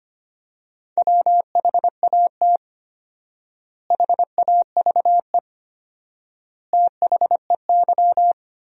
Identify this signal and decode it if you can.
Morse code, heard as a keyed tone, about 25 wpm, 700 Hz. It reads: WHAT HAVE THEY
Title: 4x Word-Spacing - Sentences from Top 500 Words - 25wpm